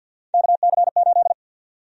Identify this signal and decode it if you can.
Morse code, heard as a keyed tone, about 52 wpm, 700 Hz. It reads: KC8
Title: USA Call Sign Prefixes - Repeated 2x - 50wpm